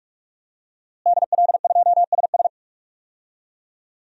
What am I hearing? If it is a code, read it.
Morse code, heard as a keyed tone, about 46 wpm, 700 Hz. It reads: DL2SS
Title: DXCC-100 Call Signs - 5 Characters - Repeated 3x - 45wpm